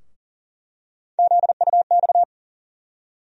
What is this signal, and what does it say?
Morse code, heard as a keyed tone, about 40 wpm, 700 Hz. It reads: ZUX
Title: USA Call Sign Postfixes - 3 Letters 40wpm